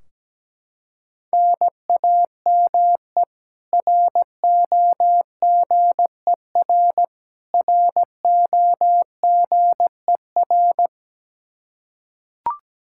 Morse code, heard as a keyed tone, about 17 wpm, 700 Hz. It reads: NAME ROGER ROGER
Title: Practice QSO - Exchange 1 - Beginning - 17wpm